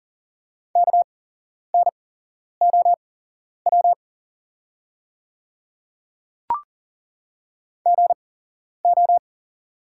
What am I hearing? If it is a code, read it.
Morse code, heard as a keyed tone, about 40 wpm, spaced as for 11 wpm, 690 Hz. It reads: KNOW GO
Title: 8x Character Spacing - Top 300 Words - 40wpm